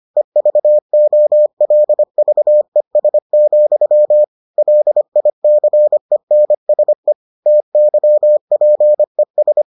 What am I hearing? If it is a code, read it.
Morse code, heard as a keyed tone, about 25 wpm, 600 Hz. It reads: EVOLVES, LICENSE TYPES